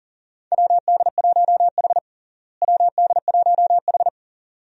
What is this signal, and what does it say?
Morse code, heard as a keyed tone, about 40 wpm, 700 Hz. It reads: WD1H WD1H
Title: USA Call Signs - 4 Characters - Repeated 2x - 40wpm